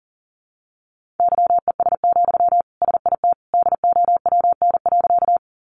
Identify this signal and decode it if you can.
Morse code, heard as a keyed tone, about 37 wpm, 700 Hz. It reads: YES, SIT DOWN.
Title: Sentences from Top 500 Words 40wpm